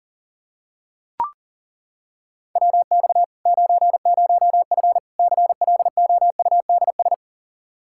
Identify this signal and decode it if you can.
Morse code, heard as a keyed tone, about 40 wpm, 700 Hz. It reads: WX 90F CLOUDS
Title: Practice QSO - Exchange 2 - Info - 40wpm